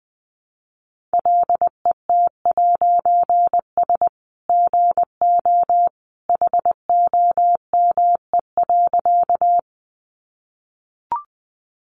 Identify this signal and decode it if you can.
Morse code, heard as a keyed tone, about 19 wpm, 700 Hz. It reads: LET'S GO HOME.
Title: A Sound Of Thunder 20wpm